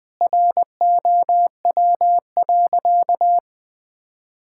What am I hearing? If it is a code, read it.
Morse code, heard as a keyed tone, about 20 wpm, 700 Hz. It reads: ROW.
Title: Sentences with 7-Word Phrases - 20wpm